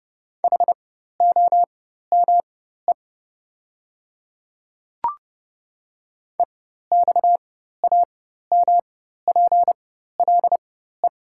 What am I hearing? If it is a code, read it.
Morse code, heard as a keyed tone, about 30 wpm, spaced as for 14 wpm, 690 Hz. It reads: HOME EXAMPLE
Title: 4x Character Spacing - Top 500 Words - 30wpm